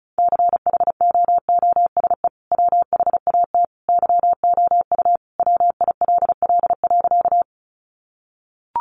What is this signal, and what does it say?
Morse code, heard as a keyed tone, about 34 wpm, 700 Hz. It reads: CHOOSE WHAT YOU WILL.
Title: Sentences from Top 700 Words 35wpm